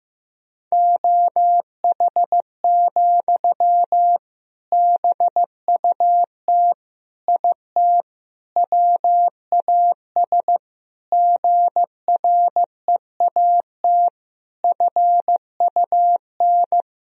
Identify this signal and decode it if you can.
Morse code, heard as a keyed tone, about 15 wpm, 700 Hz. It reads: OH, BUT IT WAS GREAT FUN
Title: Sentences from Top 1000 Words - Review Part 4 - 15wpm